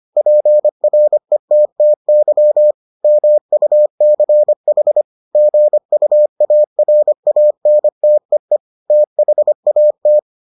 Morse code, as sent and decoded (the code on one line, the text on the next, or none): .--. .-. - - -.-- / -- ..- -.-. .... / --. ..- .- .-. .- -. - . / - .... .- -
PRETTY MUCH GUARANTEE THAT